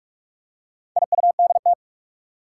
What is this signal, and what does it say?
Morse code, heard as a keyed tone, about 46 wpm, 700 Hz. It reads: IUDT
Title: Four Challenging Letter-Number Combinations 45wpm